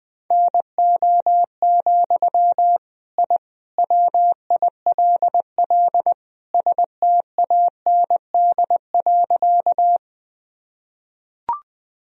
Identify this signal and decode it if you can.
Morse code, heard as a keyed tone, about 20 wpm, 700 Hz. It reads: NO, I WILL STAND.
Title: Sentences from Top 1000 Words - Review Part 1 - 20wpm